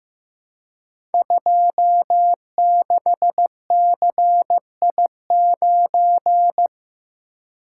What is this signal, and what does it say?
Morse code, heard as a keyed tone, about 15 wpm, 700 Hz. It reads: 26CI9
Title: Code Groups - 5 Letter:Numbers - 15wpm